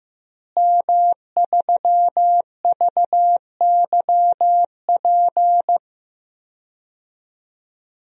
Morse code, heard as a keyed tone, about 15 wpm, 700 Hz. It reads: M3VYP
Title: DXCC-100 Call Signs - 5 Characters - Repeated 3x - 15wpm